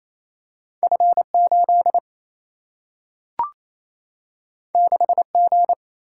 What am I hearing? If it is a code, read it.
Morse code, heard as a keyed tone, about 28 wpm, 700 Hz. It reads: F8 6G
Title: Single-Letter Single-Number 28wpm